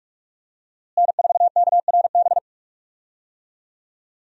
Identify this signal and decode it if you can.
Morse code, heard as a keyed tone, about 46 wpm, 700 Hz. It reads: N4KRB
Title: USA Call Signs - 5 Characters - Repeated 2x - 45wpm